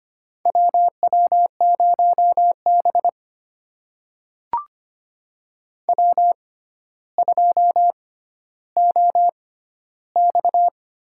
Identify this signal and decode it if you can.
Morse code, heard as a keyed tone, about 25 wpm, spaced as for 13 wpm, 700 Hz. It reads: WW0B W2OX
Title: USA Call Signs - 4 Characters - 6x Character Spacing - 25wpm